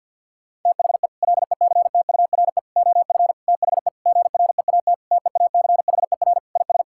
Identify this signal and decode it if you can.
Morse code, heard as a keyed tone, about 50 wpm, 700 Hz. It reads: THE LECTURE OF THE GREAT TEACHER IS
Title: Sentences from Top 1600 Words 50wpm